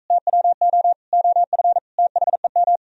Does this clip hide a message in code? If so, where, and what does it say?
Yes: Morse code from 0.1 to 2.8 s, heard as a keyed tone, about 44 wpm, 700 Hz. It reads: TWO OF THEM